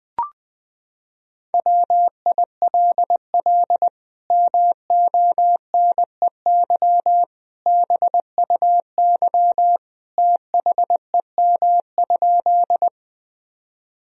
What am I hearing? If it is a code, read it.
Morse code, heard as a keyed tone, about 20 wpm, 700 Hz. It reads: WILL MONEY BUY THEM?
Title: Sentences from Top 1000 Words - Review Part 4 - 20wpm